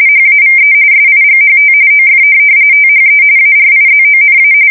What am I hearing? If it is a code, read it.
RTTY, about 45.45 baud. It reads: HAVE SIGHT OF THE ARIZONA
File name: fsk170 / USS-Missouri